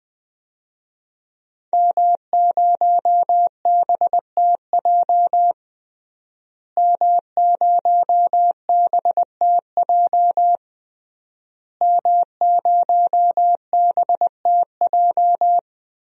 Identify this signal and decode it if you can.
Morse code, heard as a keyed tone, about 21 wpm, 700 Hz. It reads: M0BTJ M0BTJ M0BTJ
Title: DXCC-100 Call Signs - 5 Characters - Repeated 3x - 20wpm